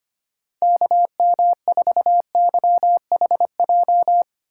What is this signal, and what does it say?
Morse code, heard as a keyed tone, about 25 wpm, 700 Hz. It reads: KM4YHJ
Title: USA Call Signs - 2x Character Spacing - 25wpm